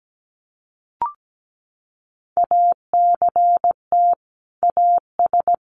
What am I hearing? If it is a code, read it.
Morse code, heard as a keyed tone, about 17 wpm, 700 Hz. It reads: ACT AS
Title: Sentences from Top 1500 Words 17wpm